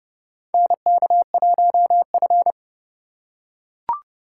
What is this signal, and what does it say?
Morse code, heard as a keyed tone, about 30 wpm, 700 Hz. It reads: NK1F
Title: USA Call Signs - 4 Characters - 3x Character Spacing - 30wpm